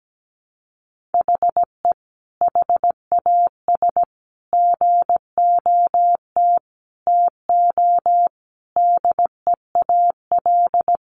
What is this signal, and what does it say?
Morse code, heard as a keyed tone, about 17 wpm, 700 Hz. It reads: HE HAS GOT TO DEAL
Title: Sentences from Top 800 Words 17wpm